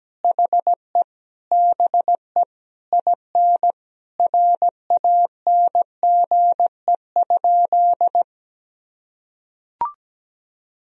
Morse code, heard as a keyed tone, about 17 wpm, 700 Hz. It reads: HE BE IN RANGE?
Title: Sentences from Top 1000 Words - Review Part 4 - 17wpm